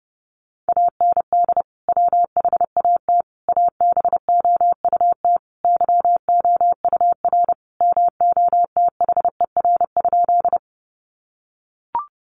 Morse code, heard as a keyed tone, about 30 wpm, 700 Hz. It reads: AND WHAT ABOUT YOUR MOTHER?
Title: Sentences from Top 200 Words 30wpm